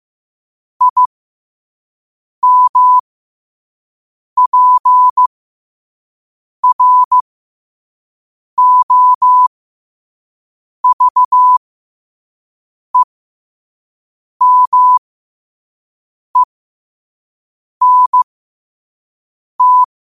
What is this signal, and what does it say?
Morse code, heard as a keyed tone, about 15 wpm, spaced as for 5 wpm, 1000 Hz. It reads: IMPROVEMENT
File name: qotdfiles / 5wpm